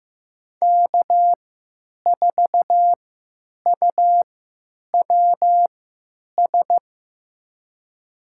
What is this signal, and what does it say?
Morse code, heard as a keyed tone, about 15 wpm, spaced as for 9 wpm, 700 Hz. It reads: K4UWS
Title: USA Call Signs - 5 Characters - 3x Character Spacing - 15wpm